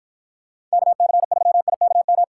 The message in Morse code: KC3IKM

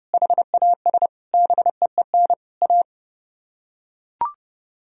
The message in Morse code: HAS BEEN A

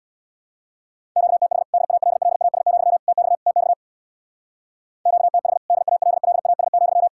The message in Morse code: QTH DALLAS, TX TX QTH DALLAS,

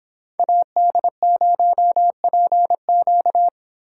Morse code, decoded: AD0PQ

